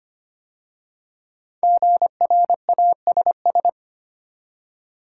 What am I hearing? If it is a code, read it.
Morse code, heard as a keyed tone, about 25 wpm, 700 Hz. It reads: GRASS
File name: 5 Letter Words - First 3 Letters Given - 25wpm